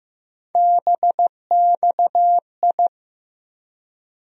Morse code, heard as a keyed tone, about 15 wpm, 700 Hz. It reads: BXI
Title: USA Call Sign Postfixes - 3 Letters 15wpm